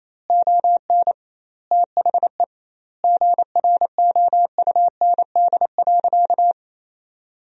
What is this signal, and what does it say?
Morse code, heard as a keyed tone, about 28 wpm, 700 Hz. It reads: ON THE GROUND.
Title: 2x Word-Spacing - Sentences from Top 500 Words - 28wpm